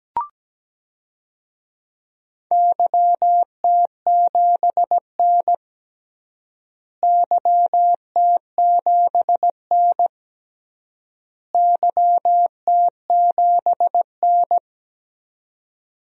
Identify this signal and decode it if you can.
Morse code, heard as a keyed tone, about 17 wpm, 700 Hz. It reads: YT7N YT7N YT7N
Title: DXCC-100 Call Signs - 4 Characters - Repeated 3x - 17wpm